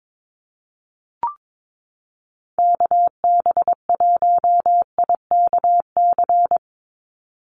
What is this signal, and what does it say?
Morse code, heard as a keyed tone, about 22 wpm, 700 Hz. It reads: KB1IKC
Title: USA Call Signs 22wpm